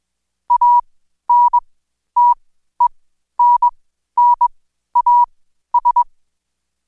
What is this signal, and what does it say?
Morse code, heard as a keyed tone, about 21 wpm, spaced as for 12 wpm, 970 Hz. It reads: ANTENNAS
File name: Code Quick / AudioDownld2 / Seg121-Day26-DayDrill 26